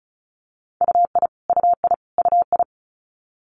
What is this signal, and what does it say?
Morse code, heard as a keyed tone, about 36 wpm, 700 Hz. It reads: UI UI UI
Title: Multi-letter Phonemes Repeated 3x 35wpm